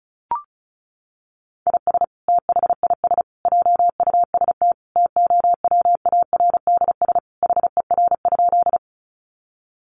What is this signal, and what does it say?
Morse code, heard as a keyed tone, about 34 wpm, 700 Hz. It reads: IS THIS JUST TOWARDS HER?
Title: Sentences from Top 1100 Words 35wpm